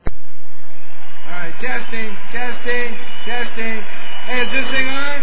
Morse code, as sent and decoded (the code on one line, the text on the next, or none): none